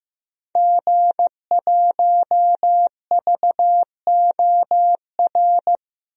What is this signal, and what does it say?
Morse code, heard as a keyed tone, about 15 wpm, 700 Hz. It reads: G1VOR
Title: DXCC-100 Call Signs - 5 Characters - Repeated 2x - 15wpm